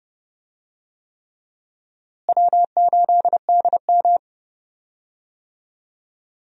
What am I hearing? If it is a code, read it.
Morse code, heard as a keyed tone, about 30 wpm, 700 Hz. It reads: W8DM
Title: USA Call Signs - 4 Characters - Repeated 3x - 30wpm